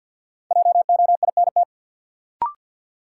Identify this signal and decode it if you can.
Morse code, heard as a keyed tone, about 48 wpm, 700 Hz. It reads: JOINT
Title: Sentences from Top 1300 Words - New Word Drill - 50wpm